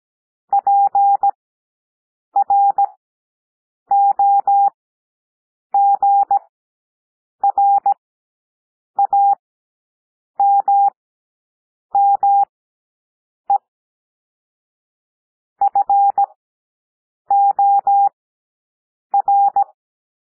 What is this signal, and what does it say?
Morse code, heard as a keyed tone, about 17 wpm, spaced as for 7 wpm, 800 Hz. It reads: PROGRAMME FOR
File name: SOUND PLAIN 17 wpm